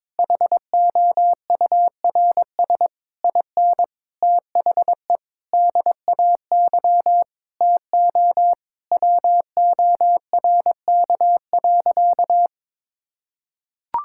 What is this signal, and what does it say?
Morse code, heard as a keyed tone, about 22 wpm, 700 Hz. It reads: HOURS IN THE DAY TO WORK.
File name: Sentences with 9-Word Phrases - 22wpm